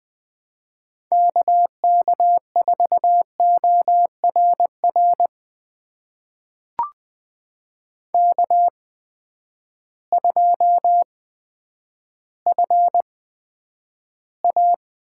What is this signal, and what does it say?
Morse code, heard as a keyed tone, about 20 wpm, spaced as for 9 wpm, 700 Hz. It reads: K K4ORR K2FA
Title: USA Call Signs - 8x Character Spacing - 20wpm